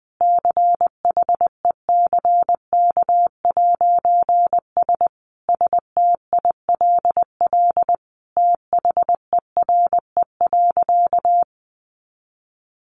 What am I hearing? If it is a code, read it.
Morse code, heard as a keyed tone, about 20 wpm, 700 Hz. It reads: CHECK'S STILL THERE.
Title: A Sound Of Thunder 20wpm